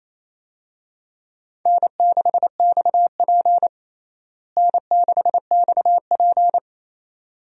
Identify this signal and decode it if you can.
Morse code, heard as a keyed tone, about 28 wpm, 700 Hz. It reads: N6XP N6XP
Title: USA Call Signs - 4 Characters - Repeated 2x - 28wpm